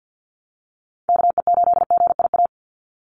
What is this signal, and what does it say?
Morse code, heard as a keyed tone, about 50 wpm, 700 Hz. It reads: KE8GIA
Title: USA Call Signs 50wpm